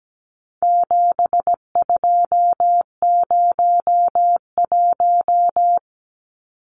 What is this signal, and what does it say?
Morse code, heard as a keyed tone, about 17 wpm, 700 Hz. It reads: 7201
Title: Numbers - 4-Digit 17wpm